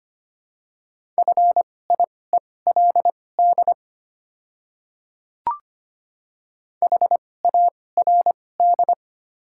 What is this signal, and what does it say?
Morse code, heard as a keyed tone, about 25 wpm, spaced as for 18 wpm, 700 Hz. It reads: FIELD HARD
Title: 2x Character Spacing - Top 500 Words - 25wpm